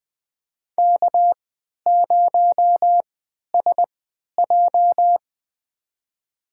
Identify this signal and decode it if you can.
Morse code, heard as a keyed tone, about 20 wpm, spaced as for 11 wpm, 700 Hz. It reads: K0SJ
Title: USA Call Signs - 4 Characters - 3x Character Spacing - 20wpm